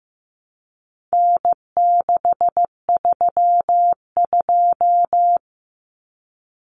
Morse code, heard as a keyed tone, about 15 wpm, 700 Hz. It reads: N632
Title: Common Cut Numbers - 4-Digit 15wpm